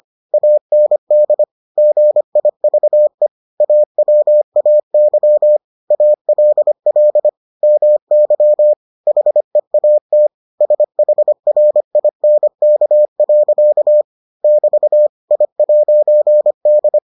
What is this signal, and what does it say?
Morse code, heard as a keyed tone, about 25 wpm, 600 Hz. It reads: AND GIVE AWAY ALL MY HEAT SHRINK. <BT> I'D